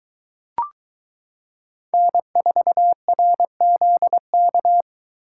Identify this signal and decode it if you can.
Morse code, heard as a keyed tone, about 23 wpm, 700 Hz. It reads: N4RZK